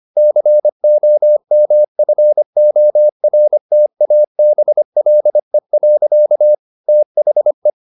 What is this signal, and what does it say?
Morse code, heard as a keyed tone, about 25 wpm, 600 Hz. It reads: COMFORTABLE. THE
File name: the old man and the sea part 1 25wpm